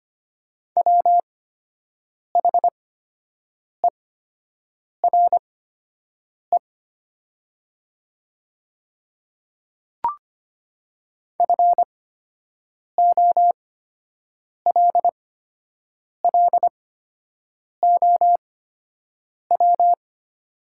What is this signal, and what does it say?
Morse code, heard as a keyed tone, about 25 wpm, spaced as for 7 wpm, 700 Hz. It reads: WHERE FOLLOW
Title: 8x Character Spacing - Top 300 Words - 25wpm